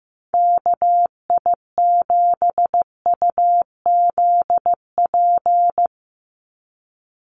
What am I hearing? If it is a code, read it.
Morse code, heard as a keyed tone, about 14 wpm, 700 Hz. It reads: KI7UZP
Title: USA Call Signs 15wpm